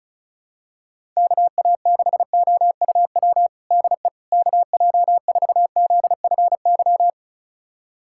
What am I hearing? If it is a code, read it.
Morse code, heard as a keyed tone, about 36 wpm, 700 Hz. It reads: KA6OUW DE KJ4ZFY
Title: USA Callsign DE Callsign - 6 Characters - 35wpm